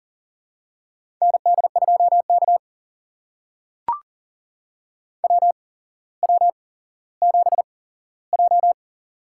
Morse code, heard as a keyed tone, about 40 wpm, spaced as for 17 wpm, 700 Hz. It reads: ND2K WW7J
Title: USA Call Signs - 8x Character Spacing - 40wpm